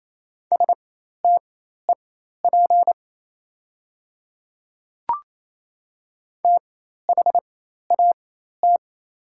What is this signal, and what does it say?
Morse code, heard as a keyed tone, about 27 wpm, spaced as for 13 wpm, 700 Hz. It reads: STEP THAT
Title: 4x Character Spacing - Top 500 Words - 28wpm